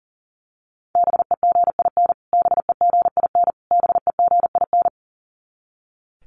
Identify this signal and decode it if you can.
Morse code, heard as a keyed tone, about 37 wpm, 700 Hz. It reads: BEGIN BEGIN BEGIN
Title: Top 500 Words Repeated 40wpm